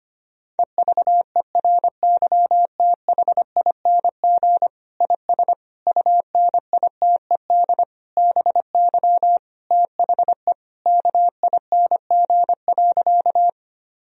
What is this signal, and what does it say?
Morse code, heard as a keyed tone, about 25 wpm, 700 Hz. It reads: EVERYTHING IS UNITED BY THE KING.
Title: Sentences from Top 1000 Words - Review Part 4 - 25wpm